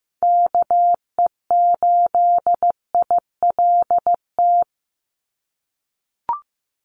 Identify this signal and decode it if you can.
Morse code, heard as a keyed tone, about 15 wpm, 700 Hz. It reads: KE8ILT
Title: USA Call Signs 15wpm